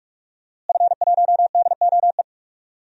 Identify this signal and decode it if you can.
Morse code, heard as a keyed tone, about 48 wpm, 700 Hz. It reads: F1DOE
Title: DXCC-100 Call Signs - 5 Characters - Repeated 2x - 45wpm